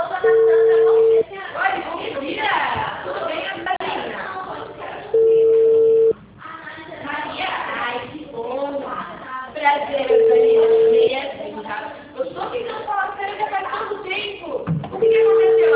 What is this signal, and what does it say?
Ringback tone (China, 450 Hz), with speech.